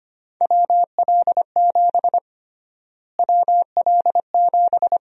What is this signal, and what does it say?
Morse code, heard as a keyed tone, about 25 wpm, 700 Hz. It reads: WL7 WL7
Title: USA Call Sign Prefixes - Repeated 2x - 25wpm